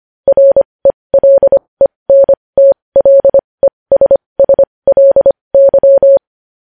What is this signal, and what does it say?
Morse code, heard as a keyed tone, about 25 wpm, spaced as for 20 wpm, 550 Hz. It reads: RELENTLESSLY
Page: news-headlines-in-morse-code-at-20-wpm